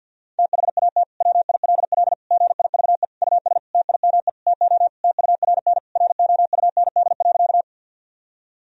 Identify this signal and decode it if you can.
Morse code, heard as a keyed tone, about 52 wpm, 700 Hz. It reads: THAT WILL GIVE US TIME TO TURN ROUND.
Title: Sentences from Top 1000 Words - Review Part 1 - 50wpm